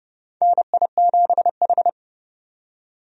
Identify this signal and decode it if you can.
Morse code, heard as a keyed tone, about 32 wpm, 700 Hz. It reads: NI7H
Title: USA Call Signs - 4 Characters - 30wpm